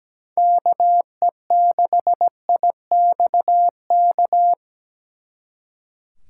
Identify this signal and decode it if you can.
Morse code, heard as a keyed tone, about 20 wpm, 700 Hz. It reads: KE6IXK
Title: Speed Racing - USA 2x3 Call Signs 17wpm